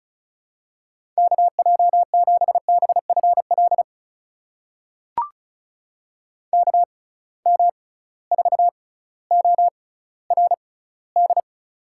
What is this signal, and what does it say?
Morse code, heard as a keyed tone, about 36 wpm, spaced as for 18 wpm, 700 Hz. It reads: KJ7BFL KM4ORD
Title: USA Call Signs - 6 Characters - 6x Character Spacing - 35wpm